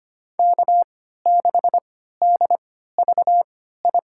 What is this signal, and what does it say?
Morse code, heard as a keyed tone, about 25 wpm, spaced as for 14 wpm, 700 Hz. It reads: K6DVI
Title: USA Call Signs - 5 Characters - 3x Character Spacing - 25wpm